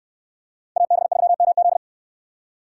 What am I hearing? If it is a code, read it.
Morse code, heard as a keyed tone, about 65 wpm, 700 Hz. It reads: AB3MZ